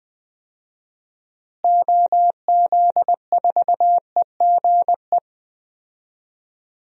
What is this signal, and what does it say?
Morse code, heard as a keyed tone, about 21 wpm, 700 Hz. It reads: OZ4EGE
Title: DXCC-100 Call Signs - 6 Characters - Repeated 3x - 20wpm